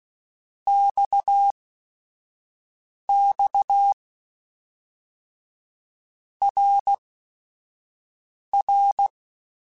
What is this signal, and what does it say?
Morse code, heard as a keyed tone, about 16 wpm, spaced as for 5 wpm, 800 Hz. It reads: XX RR
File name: Basic Sets / Basic Letter Groups Collective7 / RKPX@5